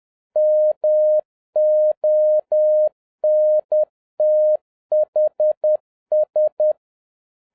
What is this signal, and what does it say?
Morse code, heard as a keyed tone, about 10 wpm, 610 Hz. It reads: MONTHS